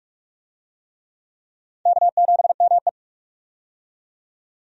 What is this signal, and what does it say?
Morse code, heard as a keyed tone, about 46 wpm, 700 Hz. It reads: K7ME
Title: USA Call Signs - 4 Characters - 45wpm